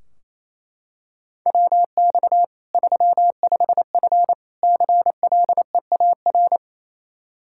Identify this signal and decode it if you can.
Morse code, heard as a keyed tone, about 27 wpm, 700 Hz. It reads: WX 35F CLEAR
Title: K1USN SST QSO Practice 28wpm